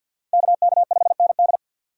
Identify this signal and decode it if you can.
Morse code, heard as a keyed tone, about 53 wpm, 690 Hz. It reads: KK5ND
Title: USA Call Signs - 5 Characters - 6x Character Spacing - 50wpm